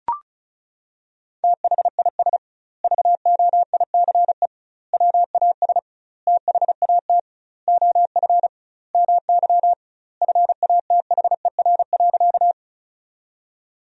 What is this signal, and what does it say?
Morse code, heard as a keyed tone, about 36 wpm, 700 Hz. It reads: THIS VOICE WAS THAT OF MY FATHER.
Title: 2x Word-Spacing - Sentences from Top 500 Words - 35wpm